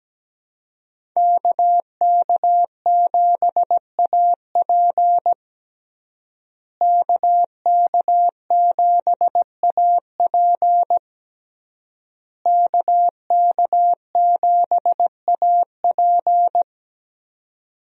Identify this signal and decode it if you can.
Morse code, heard as a keyed tone, about 17 wpm, 700 Hz. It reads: KK7AP KK7AP KK7AP
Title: USA Call Signs - 5 Characters - Repeated 3x - 17wpm